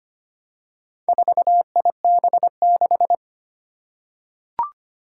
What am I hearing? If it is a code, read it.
Morse code, heard as a keyed tone, about 25 wpm, 700 Hz. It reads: 4IB6
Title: Four Challenging Letter-Number Combinations 25wpm